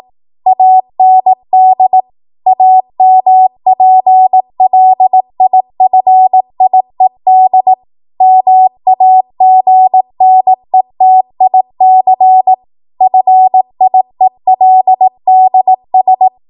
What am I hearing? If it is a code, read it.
Morse code, heard as a keyed tone, about 18 wpm, 750 Hz. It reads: AND AMPLIFIED MAGNETIC FIELDS
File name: Morse / 18WPM